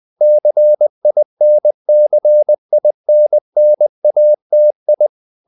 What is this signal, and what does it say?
Morse code, heard as a keyed tone, about 20 wpm, 600 Hz. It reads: CINCINNATI
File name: the old man and the sea / the old man and the sea part 1 20wpm